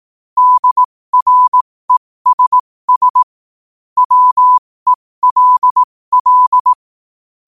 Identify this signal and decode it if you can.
Morse code, heard as a keyed tone, about 18 wpm, spaced as for 15 wpm, 1000 Hz. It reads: DRESS WELL
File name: qotdfiles / 15wpm